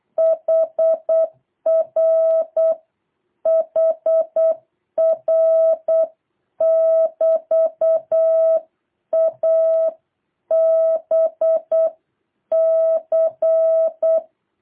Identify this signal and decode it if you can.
Morse code, heard as a keyed tone, about 8 wpm, 660 Hz. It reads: HR HR <BT> A B C
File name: jh8cbh / 2023cw